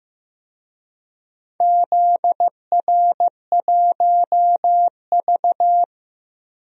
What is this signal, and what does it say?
Morse code, heard as a keyed tone, about 15 wpm, 700 Hz. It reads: ZR1V
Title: DXCC-100 Call Signs - 4 Characters - Repeated 3x - 15wpm